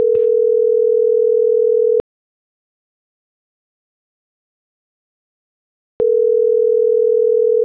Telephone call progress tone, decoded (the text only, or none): ringback tone